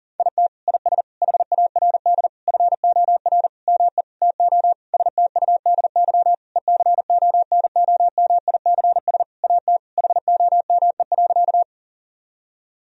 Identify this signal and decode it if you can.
Morse code, heard as a keyed tone, about 37 wpm, 700 Hz. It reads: IT IS HARD FOR ME TO STUDY ECONOMICS AT HOME.
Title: Sentences from Top 1000 Words - Review Part 3 - 40wpm